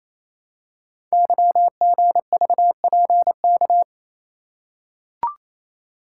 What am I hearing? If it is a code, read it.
Morse code, heard as a keyed tone, about 28 wpm, 700 Hz. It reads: YGVPK
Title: Code Groups - 5 Letters - 28wpm